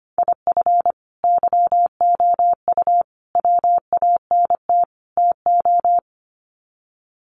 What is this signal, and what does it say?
Morse code, heard as a keyed tone, about 25 wpm, 700 Hz. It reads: IF YOU WANT TO
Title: Sets of 4 Words 25wpm